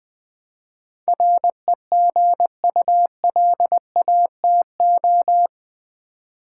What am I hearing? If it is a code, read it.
Morse code, heard as a keyed tone, about 20 wpm, 700 Hz. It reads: REGULATO
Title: Build A Word - 4 Letters At A Time - 10-Letter Words - 20wpm